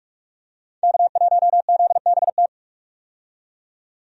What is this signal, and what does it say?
Morse code, heard as a keyed tone, about 46 wpm, 700 Hz. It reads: K1ZBT
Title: USA Call Signs - 5 Characters - 45wpm